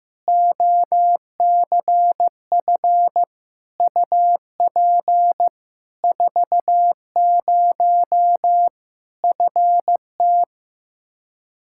Morse code, heard as a keyed tone, about 15 wpm, 700 Hz. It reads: OCF UP 40 FT
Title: Practice QSO - Exchange 2 - Info - 15wpm